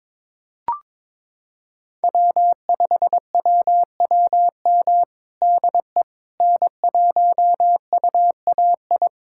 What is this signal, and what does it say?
Morse code, heard as a keyed tone, about 22 wpm, 700 Hz. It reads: W5WWM DE N1UAI
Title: USA Callsign DE Callsign - 5 Characters - 22wpm